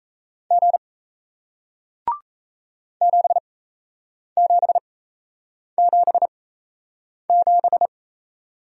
Morse code, heard as a keyed tone, about 37 wpm, spaced as for 8 wpm, 700 Hz. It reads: G 7777